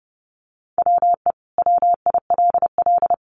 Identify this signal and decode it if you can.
Morse code, heard as a keyed tone, about 30 wpm, 700 Hz. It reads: WE WILL
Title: Sets of 5 Words Encore 30wpm